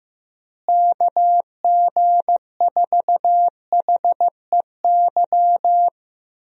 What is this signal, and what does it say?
Morse code, heard as a keyed tone, about 15 wpm, 700 Hz. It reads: KG4HEY
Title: USA Call Signs - 6 Characters - Repeated 3x - 15wpm